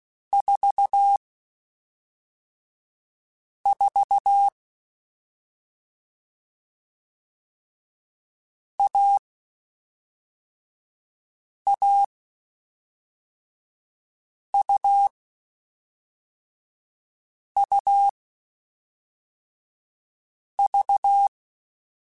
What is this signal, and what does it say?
Morse code, heard as a keyed tone, about 16 wpm, 800 Hz. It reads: 4 4 A A U U V